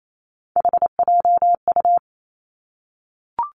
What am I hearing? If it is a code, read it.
Morse code, heard as a keyed tone, about 28 wpm, 700 Hz. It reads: HJU
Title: Three Challenging Letter-Number Combinations 28wpm